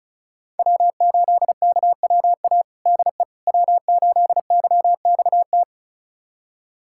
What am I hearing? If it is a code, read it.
Morse code, heard as a keyed tone, about 36 wpm, 700 Hz. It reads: W8KWA DE W8YXT
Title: USA Callsign DE Callsign - 5 Characters - 35wpm